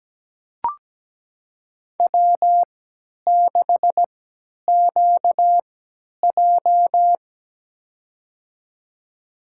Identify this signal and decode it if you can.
Morse code, heard as a keyed tone, about 17 wpm, spaced as for 10 wpm, 700 Hz. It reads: W6QJ
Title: USA Call Signs - 4 Characters - 3x Character Spacing - 17wpm